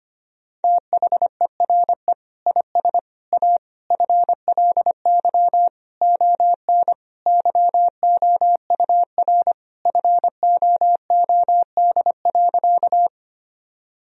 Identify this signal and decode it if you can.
Morse code, heard as a keyed tone, about 25 wpm, 700 Hz. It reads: THERE IS A FLY ON YOUR FOOD.
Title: Sentences from Top 1000 Words - Review Part 2 - 25wpm